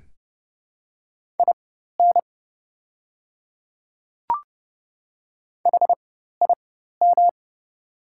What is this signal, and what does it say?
Morse code, heard as a keyed tone, about 30 wpm, spaced as for 14 wpm, 700 Hz. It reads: IN HIM
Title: 4x Character Spacing - Top 100 Words - 30wpm